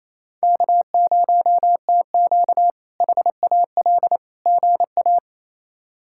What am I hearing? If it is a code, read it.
Morse code, heard as a keyed tone, about 28 wpm, 700 Hz. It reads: K0TQ HAL GA
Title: K1USN SST QSO Practice - 28wpm